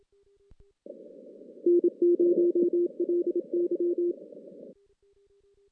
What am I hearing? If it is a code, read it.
Morse code, heard as a keyed tone, about 27 wpm, 350 Hz. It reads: N0LY
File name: 2016 JAN CW NAQP / 2016 Jan naqp k5ot - 0-4